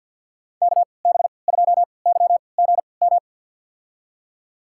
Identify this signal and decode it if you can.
Morse code, heard as a keyed tone, about 50 wpm, spaced as for 28 wpm, 700 Hz. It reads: KB2YGM
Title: USA Call Signs - 3x Character Spacing - 50wpm